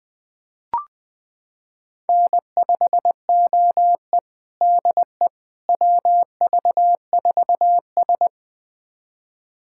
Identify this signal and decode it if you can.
Morse code, heard as a keyed tone, about 20 wpm, 700 Hz. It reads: N5OE DE WV4S
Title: USA Callsign DE Callsign - 4 Characters - 20wpm